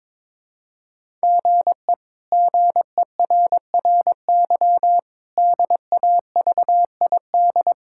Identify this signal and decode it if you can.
Morse code, heard as a keyed tone, about 22 wpm, 700 Hz. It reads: GE GERRY DAVID
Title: K1USN SST QSO Practice - 22wpm